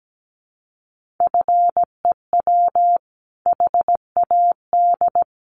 Morse code, heard as a keyed tone, about 17 wpm, 700 Hz. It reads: FEW HAD
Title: Sentences from Top 300 Words 17wpm